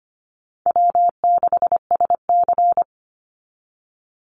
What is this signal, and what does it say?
Morse code, heard as a keyed tone, about 25 wpm, 700 Hz. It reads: W6SC
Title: Easy Call Signs Repeated 3x - 25wpm